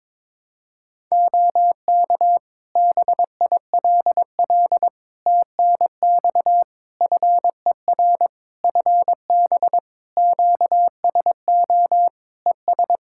Morse code, heard as a keyed tone, about 22 wpm, 700 Hz. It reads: OK BILL TNX FER FB QSO ES
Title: Practice QSO - Exchange 4 - Ending - 22wpm